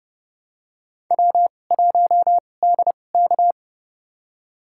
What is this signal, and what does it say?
Morse code, heard as a keyed tone, about 30 wpm, spaced as for 22 wpm, 690 Hz. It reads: W1DK
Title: USA Call Signs - 4 Characters - 2x Character Spacing - 30wpm